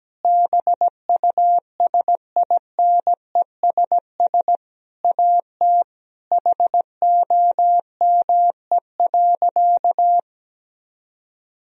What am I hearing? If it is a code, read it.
Morse code, heard as a keyed tone, about 17 wpm, 700 Hz. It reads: BUSINESS AT HOME.
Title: Sentences from Top 1000 Words - Review Part 4 - 17wpm